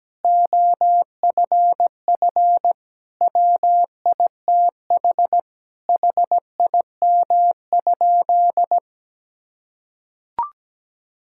Morse code, heard as a keyed tone, about 17 wpm, 700 Hz. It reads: OFF WITH HIM?
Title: Sentences from Top 1000 Words - Review Part 6 - 17wpm